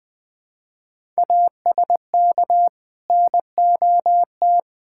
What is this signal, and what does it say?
Morse code, heard as a keyed tone, about 20 wpm, 700 Hz. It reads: ASK NOT
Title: Sentences with 9-Word Phrases - 20wpm